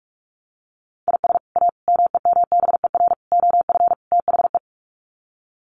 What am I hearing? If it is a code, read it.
Morse code, heard as a keyed tone, about 44 wpm, 700 Hz. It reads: IS A MEMBER OF THE
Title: Sets of 5 Words 45wpm